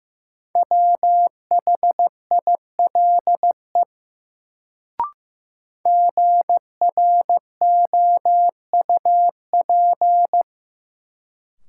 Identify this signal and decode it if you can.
Morse code, heard as a keyed tone, about 15 wpm, 700 Hz. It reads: WHILE GROUP
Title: Sets of 3 Words - New Word Drill - 15wpm